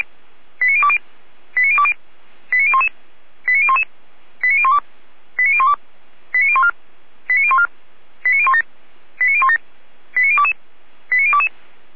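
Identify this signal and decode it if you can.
Five-tone selective call (ZVEI), heard as quick runs of five tones, tones of about 70 ms. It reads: 89020 89020 89011 89011 89012 89012 89014 89014 89017 89017 89022 89022
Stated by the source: ZVEI five-tone selective call received by SDR, feuerwehr-orlamuende.org fire brigade.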